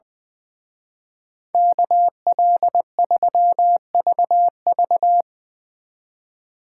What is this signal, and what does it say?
Morse code, heard as a keyed tone, about 23 wpm, 700 Hz. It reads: KL3VV
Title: Speed Racing - USA 2x2 Call Signs 20wpm